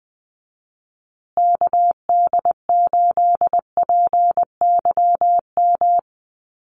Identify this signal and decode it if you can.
Morse code, heard as a keyed tone, about 20 wpm, 700 Hz. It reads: KD8PYM
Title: USA Call Signs Repeated 2x 20wpm